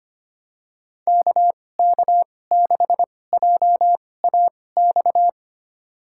Morse code, heard as a keyed tone, about 25 wpm, spaced as for 18 wpm, 700 Hz. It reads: KK6JAX
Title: USA Call Signs - 6 Characters - 2x Character Spacing - 25wpm